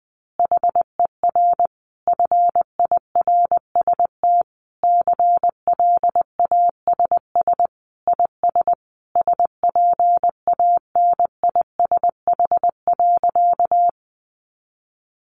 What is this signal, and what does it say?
Morse code, heard as a keyed tone, about 20 wpm, 700 Hz. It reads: HER FIRST CLASS IS SPANISH.
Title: Additional Easy Phrases 20wpm